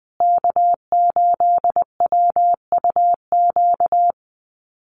Morse code, heard as a keyed tone, about 20 wpm, 700 Hz. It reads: K8WUQ
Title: USA Call Signs Repeated 3x 20wpm